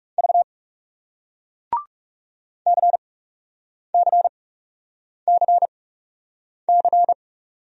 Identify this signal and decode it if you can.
Morse code, heard as a keyed tone, about 38 wpm, spaced as for 8 wpm, 700 Hz. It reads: V CCCC